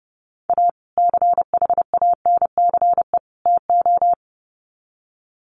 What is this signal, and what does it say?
Morse code, heard as a keyed tone, about 30 wpm, 700 Hz. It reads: A CHANCE TO
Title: Sets of 3 Words 30wpm